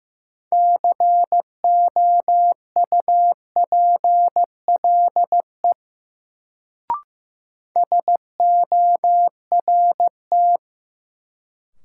Morse code, heard as a keyed tone, about 15 wpm, 700 Hz. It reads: COUPLE SORT
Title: Sets of 2 Words - New Word Drill - 15wpm